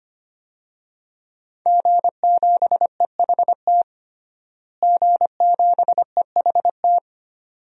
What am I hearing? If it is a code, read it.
Morse code, heard as a keyed tone, about 25 wpm, 700 Hz. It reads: G7EHT G7EHT
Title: DXCC-100 Call Signs - 5 Characters - Repeated 2x - 25wpm